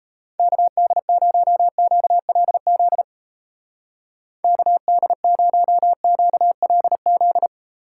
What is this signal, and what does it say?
Morse code, heard as a keyed tone, about 34 wpm, 700 Hz. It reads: KD0QLZ KD0QLZ
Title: Speed Racing - USA 2x3 Call Signs 28wpm